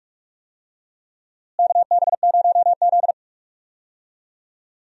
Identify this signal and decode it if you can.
Morse code, heard as a keyed tone, about 42 wpm, 700 Hz. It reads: KB0Z
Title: USA Call Signs - 4 Characters - Repeated 2x - 45wpm